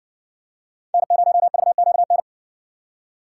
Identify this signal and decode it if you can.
Morse code, heard as a keyed tone, about 57 wpm, 700 Hz. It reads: N0VQN